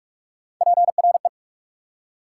Morse code, heard as a keyed tone, about 60 wpm, spaced as for 52 wpm, 690 Hz. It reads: PRE
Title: Speed-Racing Prefixes and Suffixes 45wpm